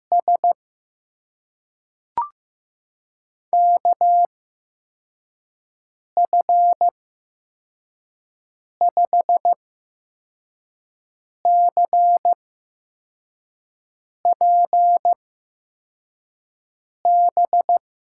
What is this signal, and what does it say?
Morse code, heard as a keyed tone, about 15 wpm, spaced as for 4 wpm, 700 Hz. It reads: S KF5CPB
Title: USA Call Signs - 8x Character Spacing - 15wpm